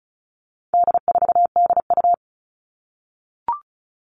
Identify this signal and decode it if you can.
Morse code, heard as a keyed tone, about 34 wpm, 700 Hz. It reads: D4DU
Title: All Cut Numbers - 4-Digit 35wpm